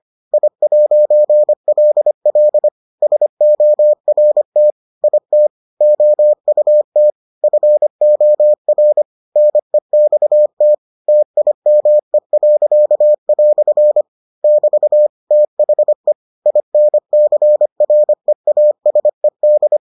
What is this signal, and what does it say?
Morse code, heard as a keyed tone, about 25 wpm, 600 Hz. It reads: I'LL SORT IT OUT FOR NEXT TIME." <BT> THE INCREASED